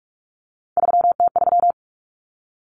Morse code, heard as a keyed tone, about 44 wpm, 700 Hz. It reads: 3T3
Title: Common Cut Numbers - 3-Digit 45wpm